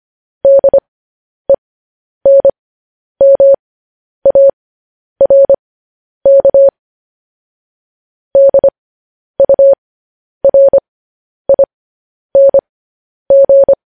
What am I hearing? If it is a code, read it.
Morse code, heard as a keyed tone, about 25 wpm, spaced as for 10 wpm, 550 Hz. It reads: DENMARK DURING
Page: news-headlines-in-morse-code-at-10-wpm